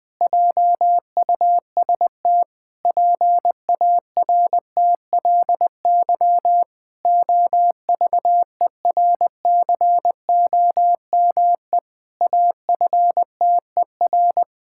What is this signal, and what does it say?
Morse code, heard as a keyed tone, about 20 wpm, 700 Hz. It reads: JUST PARTLY OVERCOME AFTER
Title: Pangram Challenge v2 - 20wpm